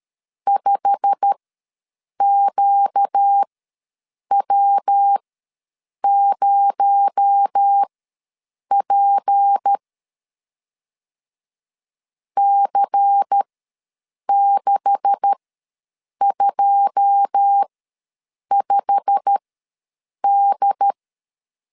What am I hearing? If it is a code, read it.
Morse code, heard as a keyed tone, about 13 wpm, spaced as for 7 wpm, 800 Hz. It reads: 5QW0P C625D
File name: LEZIONI / 9 286 ws 12wpm 3351